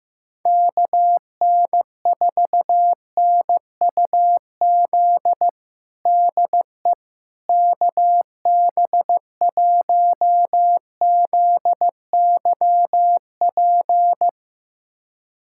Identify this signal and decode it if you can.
Morse code, heard as a keyed tone, about 15 wpm, 700 Hz. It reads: KN4NUZ DE KB1ZYP
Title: USA Callsign DE Callsign 15wpm